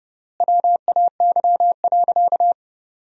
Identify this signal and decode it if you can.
Morse code, heard as a keyed tone, about 30 wpm, 700 Hz. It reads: WAY.
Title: 3x Word-Spacing - Sentences from Top 300 Words - 30wpm